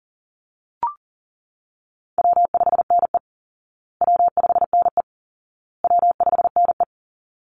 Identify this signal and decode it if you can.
Morse code, heard as a keyed tone, about 37 wpm, 700 Hz. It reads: W5NE W5NE W5NE
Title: Call Signs Repeated 2x - 40wpm